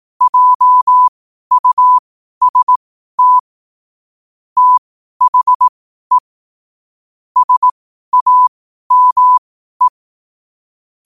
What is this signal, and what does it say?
Morse code, heard as a keyed tone, about 18 wpm, spaced as for 12 wpm, 1000 Hz. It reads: JUST THE SAME
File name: qotdfiles / 12wpm